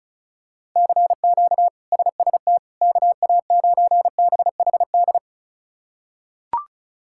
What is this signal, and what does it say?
Morse code, heard as a keyed tone, about 36 wpm, 700 Hz. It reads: CQ SST KA9BHD
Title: K1USN SST QSO Practice - 35wpm